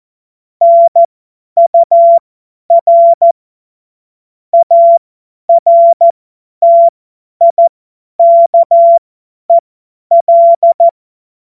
Morse code, heard as a keyed tone, about 14 wpm, spaced as for 10 wpm, 680 Hz. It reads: NUR ARTIKEL
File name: Morsekurs / Witze / Witz 09 April 2024 50 F70